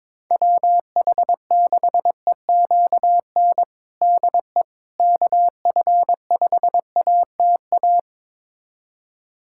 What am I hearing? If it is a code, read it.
Morse code, heard as a keyed tone, about 22 wpm, 700 Hz. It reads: WH6EQN DE KF5ATA
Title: USA Callsign DE Callsign - 6 Characters - 22wpm